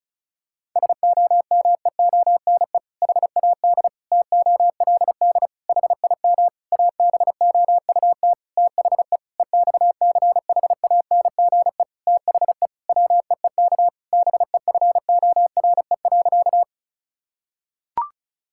Morse code, heard as a keyed tone, about 36 wpm, 700 Hz. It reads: SOMEONE HAD TOLD HIM ABOUT THE EXCHANGE THE WEEK BEFORE.
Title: Sentences from Top 1000 Words - Review Part 6 - 35wpm